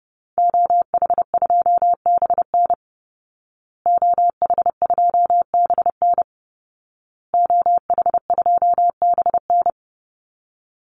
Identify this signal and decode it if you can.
Morse code, heard as a keyed tone, about 30 wpm, 700 Hz. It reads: OH2BN OH2BN OH2BN
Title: Call Signs Repeated 2x - 30wpm